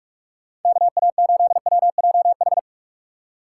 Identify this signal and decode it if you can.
Morse code, heard as a keyed tone, about 46 wpm, 700 Hz. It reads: KA8WJH
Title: USA Call Signs - 6 Characters - 2x Character Spacing - 45wpm